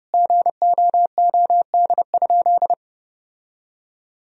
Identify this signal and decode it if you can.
Morse code, heard as a keyed tone, about 32 wpm, 700 Hz. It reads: GOOD?
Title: Sentences from Top 1000 Words - Review Part 3 - 30wpm